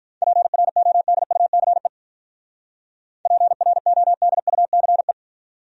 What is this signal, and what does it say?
Morse code, heard as a keyed tone, about 44 wpm, 700 Hz. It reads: PRODUCE PRODUCE